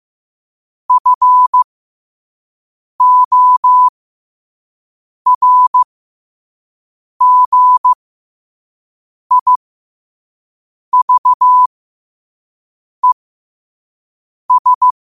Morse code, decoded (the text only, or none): FORGIVES